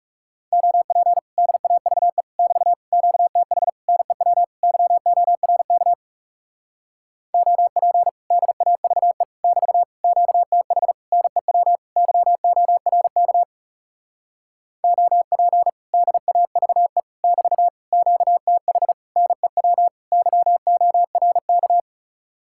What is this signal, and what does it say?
Morse code, heard as a keyed tone, about 41 wpm, 700 Hz. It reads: OP DAVE <BT> QTH NEW YORK OP DAVE <BT> QTH NEW YORK OP DAVE <BT> QTH NEW YORK